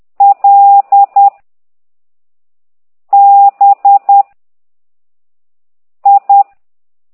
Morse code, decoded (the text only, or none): L B I